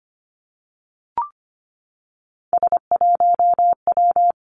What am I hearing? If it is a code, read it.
Morse code, heard as a keyed tone, about 25 wpm, 700 Hz. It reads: S1W
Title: Three Challenging Letter-Number Combinations 25wpm